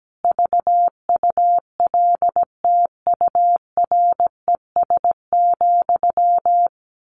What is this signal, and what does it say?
Morse code, heard as a keyed tone, about 17 wpm, 700 Hz. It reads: VULTURES,
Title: A Sound Of Thunder 17wpm